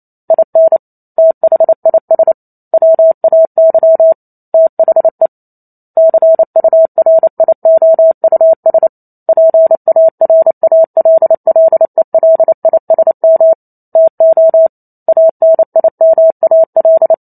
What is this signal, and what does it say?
Morse code, heard as a keyed tone, about 28 wpm, 650 Hz. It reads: IN THIS WAY THE CURIOUS PARALLELISM TO ANIMAL
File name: Chapter 2 - 28 WPM